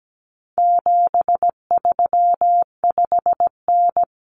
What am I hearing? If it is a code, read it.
Morse code, heard as a keyed tone, about 17 wpm, 700 Hz. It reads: 735N
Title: Common Cut Numbers - 4-Digit 17wpm